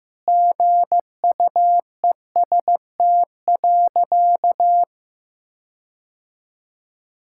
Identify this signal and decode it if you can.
Morse code, heard as a keyed tone, about 14 wpm, 700 Hz. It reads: GUEST.
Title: Sentences from Top 1600 Words 15wpm